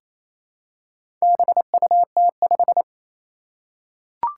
Morse code, heard as a keyed tone, about 27 wpm, 700 Hz. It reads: BUT5
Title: Four Challenging Letter-Number Combinations 28wpm